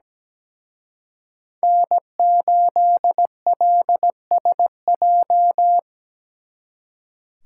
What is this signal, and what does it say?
Morse code, heard as a keyed tone, about 20 wpm, 700 Hz. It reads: N8LSJ
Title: Speed Racing - USA Call Signs 17wpm